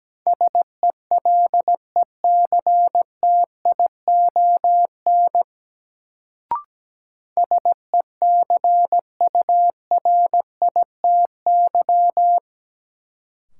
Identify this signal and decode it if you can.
Morse code, heard as a keyed tone, about 17 wpm, 700 Hz. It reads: SELECTION SECURITY
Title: Sentences from Top 1400 Words - New Word Drill Alt - 17wpm